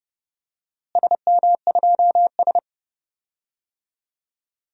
Morse code, heard as a keyed tone, about 30 wpm, 700 Hz. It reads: SM2S
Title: DXCC-100 Call Signs - 4 Characters - Repeated 3x - 30wpm